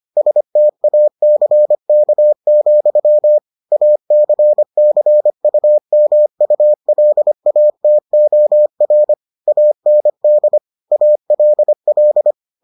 Morse code, heard as a keyed tone, about 25 wpm, 600 Hz. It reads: STACK, ACCUMULATOR AND ALL